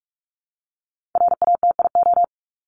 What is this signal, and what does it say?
Morse code, heard as a keyed tone, about 44 wpm, 700 Hz. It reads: RATIO
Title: Top 1751-2000 Words 45wpm